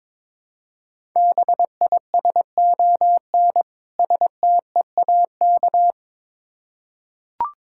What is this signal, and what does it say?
Morse code, heard as a keyed tone, about 22 wpm, 690 Hz. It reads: BISON STEAK
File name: Difficult Words with Hints - 22wpm